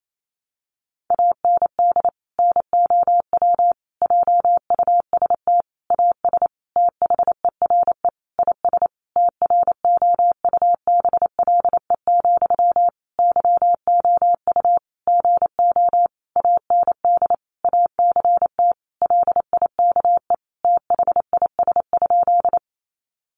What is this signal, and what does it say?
Morse code, heard as a keyed tone, about 28 wpm, 700 Hz. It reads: AND NOW JUST AS THERE IS TROUBLE, YOU GO AND ACT LIKE THIS?
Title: Sentences from Top 1200 Words 28wpm